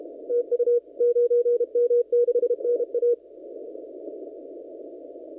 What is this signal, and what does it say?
Morse code, heard as a keyed tone, about 32 wpm, 480 Hz. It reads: TU 9M6NA
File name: CW / kt5j wpx cw-op-k5pi